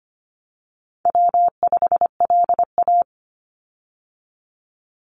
Morse code, heard as a keyed tone, about 25 wpm, 700 Hz. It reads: W5LA
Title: Call Signs - 25wpm